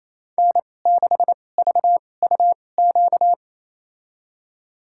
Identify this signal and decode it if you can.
Morse code, heard as a keyed tone, about 28 wpm, spaced as for 20 wpm, 700 Hz. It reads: N6VUQ
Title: USA Call Signs - 2x Character Spacing - 28wpm